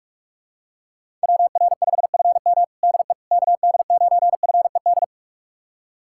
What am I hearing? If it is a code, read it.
Morse code, heard as a keyed tone, about 46 wpm, 700 Hz. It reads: WR5FM DE KD9FED
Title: USA Callsign DE Callsign 45wpm